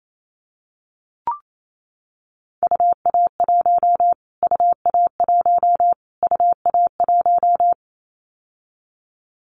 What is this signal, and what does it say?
Morse code, heard as a keyed tone, about 28 wpm, 700 Hz. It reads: UA1 UA1 UA1
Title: Callsign Prefixes for Top 25 DXCC - Repeated 3x - 28wpm